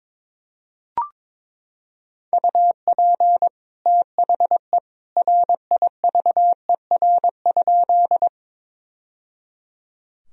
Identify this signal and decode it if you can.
Morse code, heard as a keyed tone, about 22 wpm, 700 Hz. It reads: UP THE RIVER?
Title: Sentences from Top 1000 Words - Review Part 1 - 22wpm